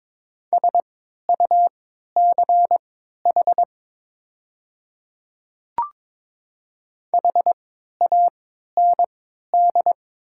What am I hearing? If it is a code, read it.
Morse code, heard as a keyed tone, about 22 wpm, spaced as for 13 wpm, 700 Hz. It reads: SUCH HAND
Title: 3x Character Spacing - Top 200 Words - 22wpm